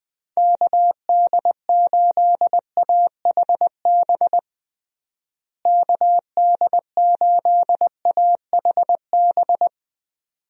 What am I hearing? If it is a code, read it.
Morse code, heard as a keyed tone, about 20 wpm, 700 Hz. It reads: KD8AHB KD8AHB
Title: USA Call Signs - 6 Characters - Repeated 2x - 20wpm